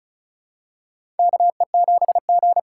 Morse code, heard as a keyed tone, about 36 wpm, 700 Hz. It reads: KE7G
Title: USA Call Signs - 3x Character Spacing - 35wpm